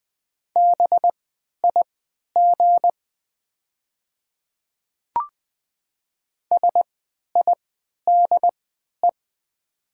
Morse code, heard as a keyed tone, about 20 wpm, spaced as for 11 wpm, 700 Hz. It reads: BIG SIDE
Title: 3x Character Spacing - Top 300 Words - 20wpm